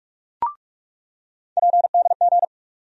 Morse code, heard as a keyed tone, about 42 wpm, 700 Hz. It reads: PDG
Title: USA Call Sign Postfixes - 3 Letters 45wpm